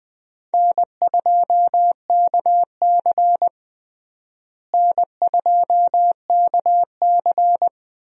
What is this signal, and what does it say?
Morse code, heard as a keyed tone, about 20 wpm, 700 Hz. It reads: N2KC N2KC
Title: USA Call Signs - 4 Characters - Repeated 3x - 20wpm